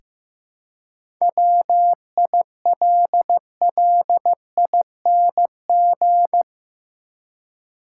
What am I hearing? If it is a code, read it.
Morse code, heard as a keyed tone, about 16 wpm, 700 Hz. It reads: WILLING
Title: Seven Letter Words - 15wpm